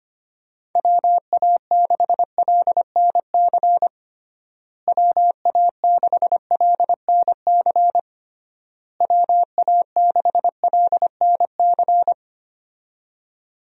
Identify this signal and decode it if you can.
Morse code, heard as a keyed tone, about 25 wpm, 700 Hz. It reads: WA6LNC WA6LNC WA6LNC
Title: USA Call Signs - 6 Characters - Repeated 3x - 25wpm